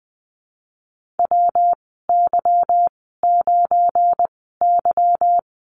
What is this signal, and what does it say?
Morse code, heard as a keyed tone, about 20 wpm, 700 Hz. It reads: W Y 9 Y
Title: 207 Lesson 33g - Letter j - Call Signs 20wpm